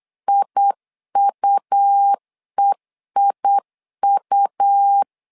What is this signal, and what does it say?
Morse code, heard as a keyed tone, about 8 wpm, 800 Hz. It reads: IUEIU
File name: LEZIONI / 3 MWRO / 3 M 8wpm 2300